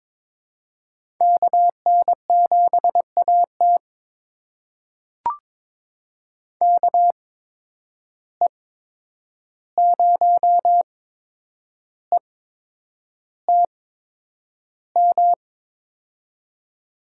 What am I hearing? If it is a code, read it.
Morse code, heard as a keyed tone, about 22 wpm, spaced as for 9 wpm, 700 Hz. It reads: KN7AT KE0ETM
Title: USA Call Signs - 8x Character Spacing - 22wpm